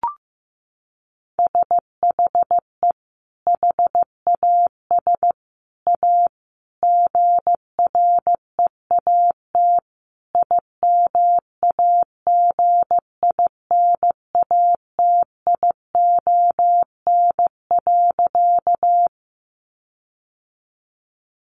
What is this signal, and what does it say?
Morse code, heard as a keyed tone, about 15 wpm, 700 Hz. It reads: SHE HAS A GREAT IMAGINATION.
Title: Sentences from Top 1300 Words 15wpm